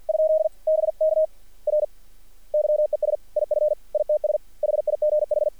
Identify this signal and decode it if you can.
Morse code, heard as a keyed tone, about 53 wpm, 620 Hz. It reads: ? BK R YES IF ITS HIGH
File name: archived / CRSnet